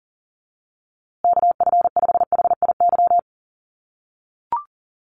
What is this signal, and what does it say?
Morse code, heard as a keyed tone, about 40 wpm, 700 Hz. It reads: KF5HIY
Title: USA Call Signs Repeated 3x 40wpm